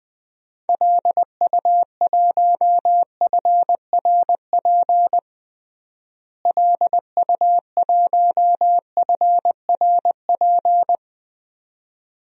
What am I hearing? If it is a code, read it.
Morse code, heard as a keyed tone, about 20 wpm, 700 Hz. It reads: LU1FRP LU1FRP
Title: DXCC-100 Call Signs - 6 Characters - Repeated 2x - 20wpm